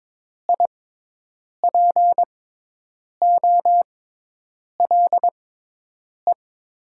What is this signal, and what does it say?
Morse code, heard as a keyed tone, about 22 wpm, spaced as for 8 wpm, 700 Hz. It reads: IPOLE
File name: Top 90 QSO Elements - 6x Character Spacing - 22wpm